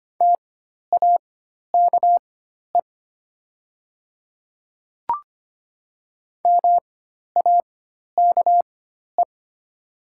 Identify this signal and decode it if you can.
Morse code, heard as a keyed tone, about 25 wpm, spaced as for 12 wpm, 690 Hz. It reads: TAKE MAKE
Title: 4x Character Spacing - Top 300 Words - 25wpm